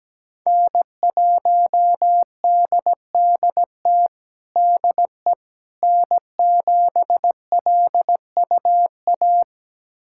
Morse code, heard as a keyed tone, about 17 wpm, 700 Hz. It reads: N1DDT DE N7LUA
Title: USA Callsign DE Callsign - 5 Characters - 17wpm